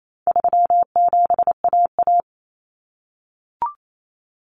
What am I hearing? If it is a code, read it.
Morse code, heard as a keyed tone, about 28 wpm, 700 Hz. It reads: 37AA
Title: Common Cut Numbers - 4-Digit 28wpm